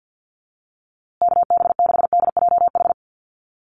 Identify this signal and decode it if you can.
Morse code, heard as a keyed tone, about 50 wpm, 700 Hz. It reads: KB6DJH
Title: USA Call Signs Repeated 3x 50wpm